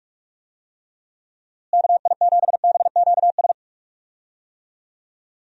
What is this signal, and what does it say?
Morse code, heard as a keyed tone, about 46 wpm, 700 Hz. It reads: KI7BQS
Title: USA Call Signs - 6 Characters - Repeated 3x - 45wpm